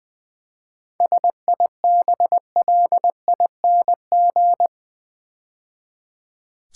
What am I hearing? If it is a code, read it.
Morse code, heard as a keyed tone, about 20 wpm, 700 Hz. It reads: SIBLING
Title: 7 Letter Words - First 3 Letters Given - 20wpm